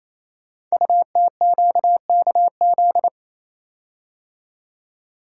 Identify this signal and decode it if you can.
Morse code, heard as a keyed tone, about 27 wpm, 700 Hz. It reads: UTQKZ
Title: Code Groups - 5 Letters - 28wpm